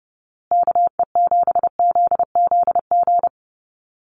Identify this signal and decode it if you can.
Morse code, heard as a keyed tone, about 30 wpm, 700 Hz. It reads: KE7ZZG
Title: USA Call Signs Repeated 2x 30wpm